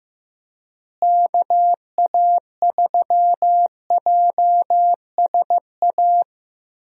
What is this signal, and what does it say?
Morse code, heard as a keyed tone, about 15 wpm, 700 Hz. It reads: KA3JSA
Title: USA Call Signs - 6 Characters - Repeated 3x - 15wpm